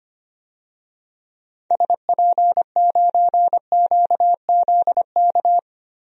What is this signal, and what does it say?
Morse code, heard as a keyed tone, about 25 wpm, 700 Hz. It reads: SP9QZK
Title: DXCC-100 Call Signs - 6 Characters - 25wpm